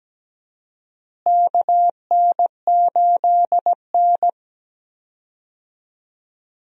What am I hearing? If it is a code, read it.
Morse code, heard as a keyed tone, about 17 wpm, 700 Hz. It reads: KN8N
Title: USA Call Signs - 4 Characters - Repeated 2x - 17wpm